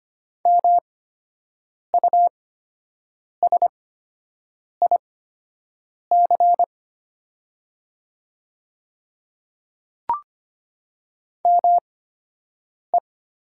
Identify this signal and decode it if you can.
Morse code, heard as a keyed tone, about 25 wpm, spaced as for 7 wpm, 700 Hz. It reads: MUSIC ME
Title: 8x Character Spacing - Top 300 Words - 25wpm